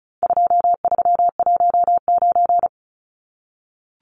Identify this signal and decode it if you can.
Morse code, heard as a keyed tone, about 34 wpm, 700 Hz. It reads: 2319
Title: Numbers - 4-Digit 35wpm